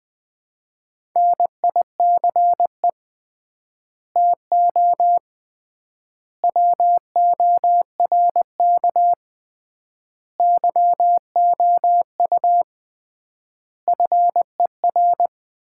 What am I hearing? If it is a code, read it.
Morse code, heard as a keyed tone, about 21 wpm, 700 Hz. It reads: NICE TO WORK YOU FER
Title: Common QSO Phrases - 3x Word-Spacing - 20wpm